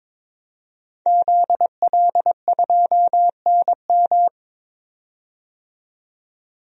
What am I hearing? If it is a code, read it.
Morse code, heard as a keyed tone, about 22 wpm, 700 Hz. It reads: ZL2NM
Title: DXCC-100 Call Signs - 5 Characters - 22wpm